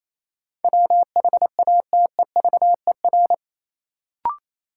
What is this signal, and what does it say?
Morse code, heard as a keyed tone, about 28 wpm, 700 Hz. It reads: WHATEVER
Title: Sentences from Top 1100 Words - New Word Drill - 28wpm